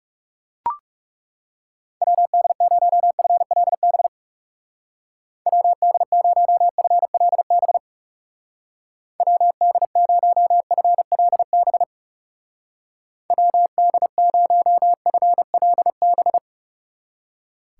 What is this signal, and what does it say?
Morse code, heard as a keyed tone, about 37 wpm, 700 Hz. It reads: WD0FLB WD0FLB WD0FLB WD0FLB
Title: Speed Racing - USA 2x3 Call Signs 30wpm